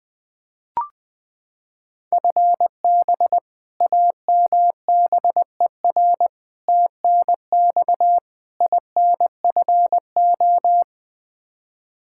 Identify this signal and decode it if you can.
Morse code, heard as a keyed tone, about 21 wpm, 700 Hz. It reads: FB AMBER TNX INFO
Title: Practice QSO - Exchange 3 - More Info - 20wpm